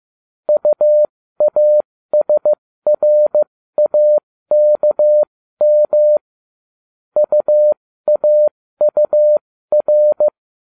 Morse code, heard as a keyed tone, about 15 wpm, spaced as for 13 wpm, 600 Hz. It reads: UASRAKM UAUR